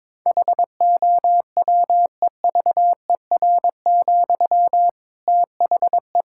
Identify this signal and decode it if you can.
Morse code, heard as a keyed tone, about 22 wpm, 700 Hz. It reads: HOWEVER, THE
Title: Sentences from Top 1000 Words - Review Part 6 - 22wpm